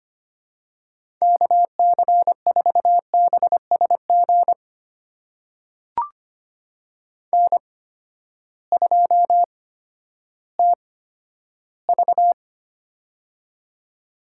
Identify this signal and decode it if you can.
Morse code, heard as a keyed tone, about 25 wpm, spaced as for 12 wpm, 700 Hz. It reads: KC4BSG N2TV
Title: USA Call Signs - 8x Character Spacing - 25wpm